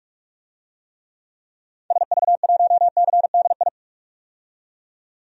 Morse code, heard as a keyed tone, about 42 wpm, 700 Hz. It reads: SV1CDI
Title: DXCC-100 Call Signs - 6 Characters - 45wpm